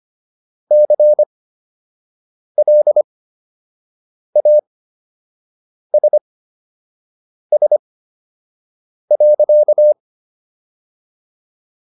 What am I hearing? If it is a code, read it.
Morse code, heard as a keyed tone, about 25 wpm, spaced as for 6 wpm, 600 Hz. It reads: CLASS.